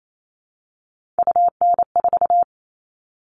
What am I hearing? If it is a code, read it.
Morse code, heard as a keyed tone, about 28 wpm, 700 Hz. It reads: UN4
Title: All Cut Numbers - 3-Digit 28wpm